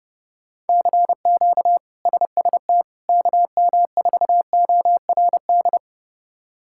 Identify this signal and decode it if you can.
Morse code, heard as a keyed tone, about 29 wpm, 700 Hz. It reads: CQ SST KM4ORD
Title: K1USN SST QSO Practice - 30wpm